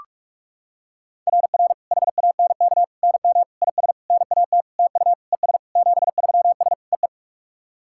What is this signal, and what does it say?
Morse code, heard as a keyed tone, about 46 wpm, 690 Hz. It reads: RR HANK NM IS NAT TU ES 73S EE
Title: SOTA QSO Practice 45wpm